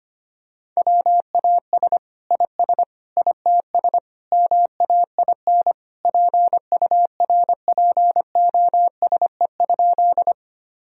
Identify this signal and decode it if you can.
Morse code, heard as a keyed tone, about 25 wpm, 700 Hz. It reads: WAS IS ITS MAIN PURPOSE?
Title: Sentences from Top 1000 Words - Review Part 2 - 25wpm